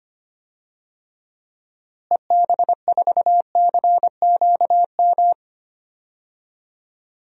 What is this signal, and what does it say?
Morse code, heard as a keyed tone, about 25 wpm, 700 Hz. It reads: EB4CQM
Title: DXCC-100 Call Signs - 6 Characters - Repeated 3x - 25wpm